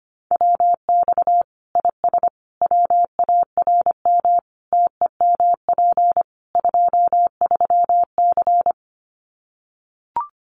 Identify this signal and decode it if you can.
Morse code, heard as a keyed tone, about 25 wpm, 700 Hz. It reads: WX IS WARM TEMP 23C
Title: Common QSO Phrases 25wpm